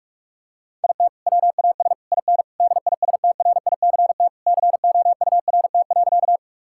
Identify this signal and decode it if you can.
Morse code, heard as a keyed tone, about 46 wpm, 700 Hz. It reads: IT WAS IN DISTRICT COURT.
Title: Sentences from Top 1600 Words 45wpm